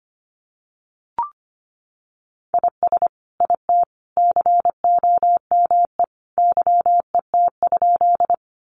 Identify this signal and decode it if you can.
Morse code, heard as a keyed tone, about 25 wpm, 700 Hz. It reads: IS IT COME YET?
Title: Sentences from Top 500 Words 25wpm